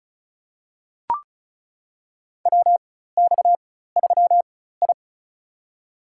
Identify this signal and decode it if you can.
Morse code, heard as a keyed tone, about 36 wpm, spaced as for 16 wpm, 700 Hz. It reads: WX3I
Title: USA Call Signs - 4 Characters - 4x Character Spacing - 35wpm